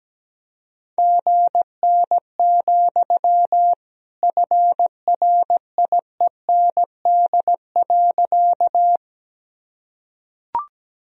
Morse code, heard as a keyed tone, about 17 wpm, 700 Hz. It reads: GN, FRIEND.
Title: Sentences from Top 300 Words and Abbreviations 17wpm